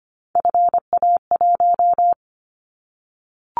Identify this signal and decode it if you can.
Morse code, heard as a keyed tone, about 25 wpm, 700 Hz. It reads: FA1
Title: Callsign Prefixes for Top 50 DXCC - 25wpm